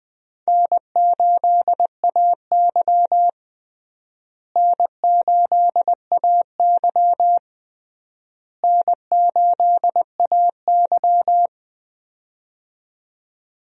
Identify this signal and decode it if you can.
Morse code, heard as a keyed tone, about 20 wpm, 700 Hz. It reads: N8AY N8AY N8AY
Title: USA Call Signs - 4 Characters - Repeated 3x - 20wpm